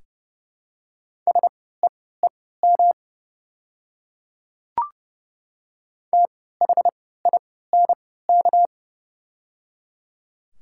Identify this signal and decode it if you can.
Morse code, heard as a keyed tone, about 31 wpm, spaced as for 17 wpm, 700 Hz. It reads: SEEM THINK
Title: 3x Character Spacing - Top 300 Words - 30wpm